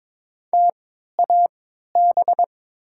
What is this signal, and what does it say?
Morse code, heard as a keyed tone, about 22 wpm, spaced as for 12 wpm, 700 Hz. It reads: TAB
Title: 3x Character Spacing - Top 500 Words - 22wpm